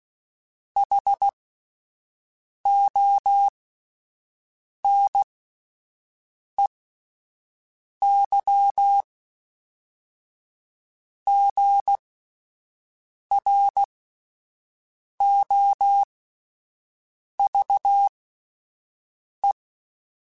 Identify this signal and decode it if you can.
Morse code, heard as a keyed tone, about 16 wpm, spaced as for 5 wpm, 800 Hz. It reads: HONEYGROVE